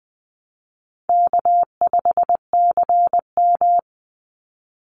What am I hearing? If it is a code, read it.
Morse code, heard as a keyed tone, about 20 wpm, 700 Hz. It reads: K5CM
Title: Easy Call Signs - 20wpm